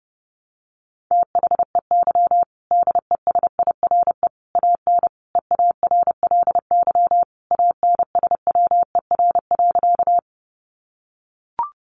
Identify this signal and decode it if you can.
Morse code, heard as a keyed tone, about 30 wpm, 700 Hz. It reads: THEY DESIRE AN EARLY ANSWER.
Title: Sentences from Top 1100 Words 30wpm